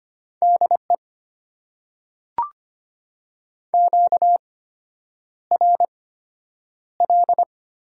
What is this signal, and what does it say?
Morse code, heard as a keyed tone, about 25 wpm, spaced as for 9 wpm, 700 Hz. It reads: DE QRL
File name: Top 90 QSO Elements - 8x Character Spacing - 25wpm